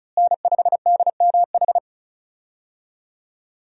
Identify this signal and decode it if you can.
Morse code, heard as a keyed tone, about 36 wpm, 700 Hz. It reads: N5DMH
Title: USA Call Signs - 5 Characters - Repeated 2x - 35wpm